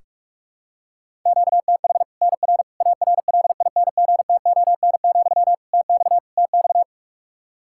Morse code, heard as a keyed tone, about 46 wpm, 700 Hz. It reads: QTH NR ARLINGTON, TX TX
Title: Practice QSO - Exchange 1 - Beginning - 45wpm